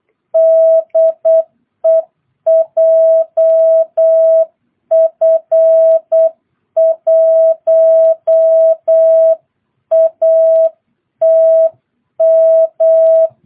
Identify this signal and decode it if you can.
Morse code, heard as a keyed tone, about 8 wpm, 650 Hz. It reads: DE JF1ATM
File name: jh8cbh / 2022cw